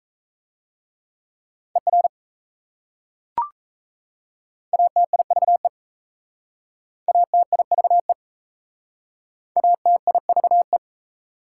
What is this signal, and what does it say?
Morse code, heard as a keyed tone, about 36 wpm, 700 Hz. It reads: ER ATIVE ATIVE ATIVE